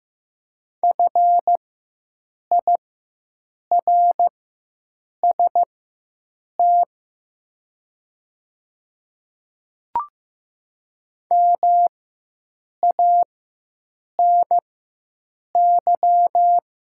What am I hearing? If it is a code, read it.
Morse code, heard as a keyed tone, about 15 wpm, spaced as for 7 wpm, 700 Hz. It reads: FIRST MANY
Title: 4x Character Spacing - Top 100 Words - 15wpm